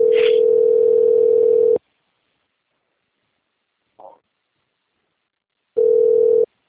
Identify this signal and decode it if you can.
Ringback tone (North America), 440 + 480 Hz.